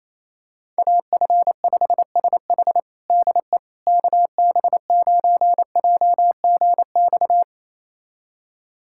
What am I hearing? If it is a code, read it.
Morse code, heard as a keyed tone, about 28 wpm, 700 Hz. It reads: AF5SH DE KB9JGX
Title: USA Callsign DE Callsign 28wpm